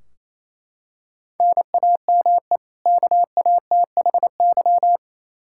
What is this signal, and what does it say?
Morse code, heard as a keyed tone, about 28 wpm, 700 Hz. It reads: NAME KATHY
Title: Practice QSO - Exchange 1 - Beginning - 28wpm